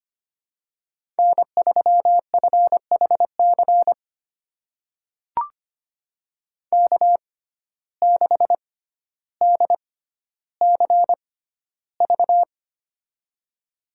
Morse code, heard as a keyed tone, about 25 wpm, spaced as for 13 wpm, 700 Hz. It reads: N3FHC K6DCV
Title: USA Call Signs - 5 Characters - 6x Character Spacing - 25wpm